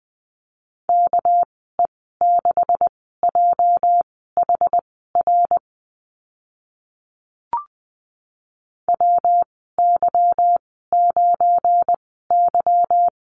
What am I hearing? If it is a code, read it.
Morse code, heard as a keyed tone, about 20 wpm, spaced as for 14 wpm, 700 Hz. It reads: KE6JHR WY9Y
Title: 219 Lesson 35g - Number 0 - Call Signs 20wpm